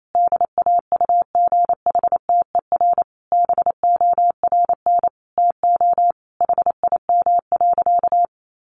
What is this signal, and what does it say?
Morse code, heard as a keyed tone, about 28 wpm, 700 Hz. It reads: DAUGHTER BORN TO HIM.
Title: Sentences from Top 1500 Words 28wpm